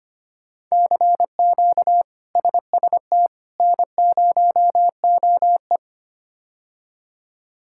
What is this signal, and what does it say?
Morse code, heard as a keyed tone, about 25 wpm, 700 Hz. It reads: CQ SST N0OE
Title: K1USN SST QSO Practice - 25wpm